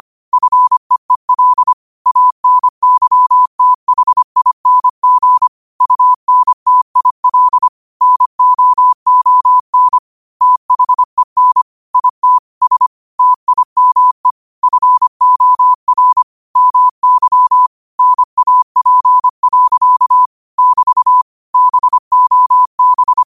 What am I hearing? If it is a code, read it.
Morse code, heard as a keyed tone, about 25 wpm, 1000 Hz. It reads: FEEL ANYTHING UNTIL NOON THEN IT S TIME FOR MY NAP. <BT> BOB
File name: qotdfiles / 25wpm